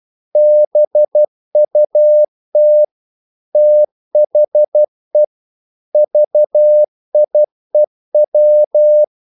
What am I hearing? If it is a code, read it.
Morse code, heard as a keyed tone, about 12 wpm, 600 Hz. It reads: BUT THE VIEW
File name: CW / Practice / EnglishQuotations0312